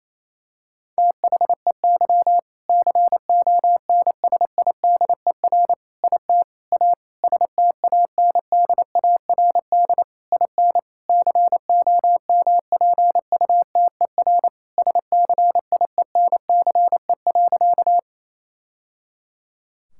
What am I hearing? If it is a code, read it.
Morse code, heard as a keyed tone, about 28 wpm, 700 Hz. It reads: THEY CONSIDER IT A STANDARD IN COMPUTER SCIENCE.
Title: Sentences from Top 1000 Words - Review Part 6 - 28wpm